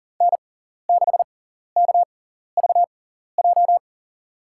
Morse code, heard as a keyed tone, about 40 wpm, spaced as for 14 wpm, 700 Hz. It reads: N6KVJ
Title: USA Call Signs - 5 Characters - 6x Character Spacing - 40wpm